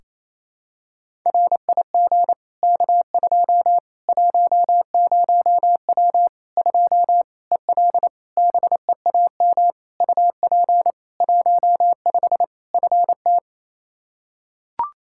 Morse code, heard as a keyed tone, about 28 wpm, 700 Hz. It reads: RIG K2 10W 2 EL BEAM UP 15 FT
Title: Practice QSO - Exchange 2 - Info - 28wpm